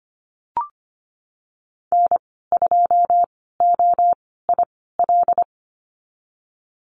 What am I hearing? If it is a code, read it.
Morse code, heard as a keyed tone, about 25 wpm, spaced as for 16 wpm, 700 Hz. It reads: N2OIL
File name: Lesson 15g - Number 5 - Call Signs 25wpm